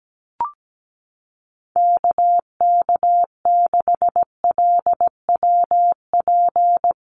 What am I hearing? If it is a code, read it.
Morse code, heard as a keyed tone, about 17 wpm, 700 Hz. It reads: KK6LWP
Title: USA Call Signs Repeated 2x 17wpm